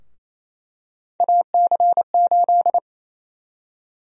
Morse code, heard as a keyed tone, about 27 wpm, 700 Hz. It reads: AC8
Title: USA Call Sign Prefixes - Repeated 2x - 28wpm